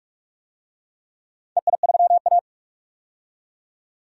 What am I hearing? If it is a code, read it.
Morse code, heard as a keyed tone, about 45 wpm, 700 Hz. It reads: EI3A
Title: DXCC-100 Call Signs - 4 Characters - 45wpm